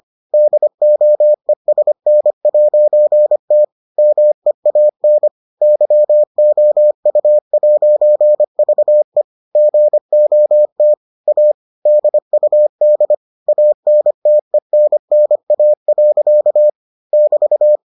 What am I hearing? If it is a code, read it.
Morse code, heard as a keyed tone, about 25 wpm, 600 Hz. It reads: DOESN'T MEAN YOU'VE GOT A DUD ANTENNA. <BT>